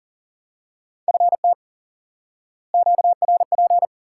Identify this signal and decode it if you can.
Morse code, heard as a keyed tone, about 40 wpm, 700 Hz. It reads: FT QRP
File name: IQE Course 3 - Lesson 89 - YAGI - Two Elements - 40wpm